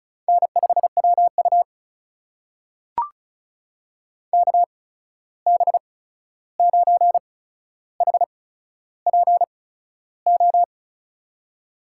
Morse code, heard as a keyed tone, about 36 wpm, spaced as for 13 wpm, 700 Hz. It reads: N5WU KB9HPO
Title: USA Call Signs - 8x Character Spacing - 35wpm